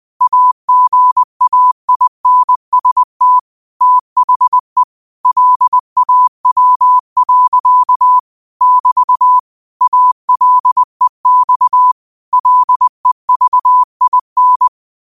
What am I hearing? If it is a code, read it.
Morse code, heard as a keyed tone, about 20 wpm, 1000 Hz. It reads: AGAINST THE LAW. <BT> ALEX LEVIN
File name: qotdfiles / 20wpm